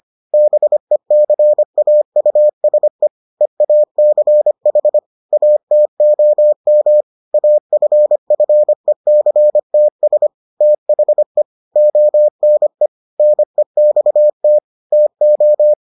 Morse code, heard as a keyed tone, about 25 wpm, 600 Hz. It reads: BECAUSE EACH ATOM AFFECTS THE ONE NEXT TO